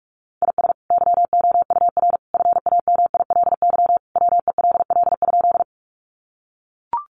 Morse code, heard as a keyed tone, about 44 wpm, 700 Hz. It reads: IS YOUR FAMILY WELL?
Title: Sentences from Top 400 Words 45wpm